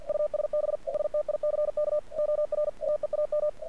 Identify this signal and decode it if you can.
Morse code, heard as a keyed tone, about 50 wpm, 600 Hz. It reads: USB STICK OR TEAM
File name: archived / CRSnet